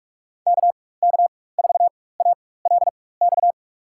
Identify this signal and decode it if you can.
Morse code, heard as a keyed tone, about 42 wpm, spaced as for 20 wpm, 700 Hz. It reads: KK4ALX
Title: USA Call Signs - 6 Characters - 4x Character Spacing - 45wpm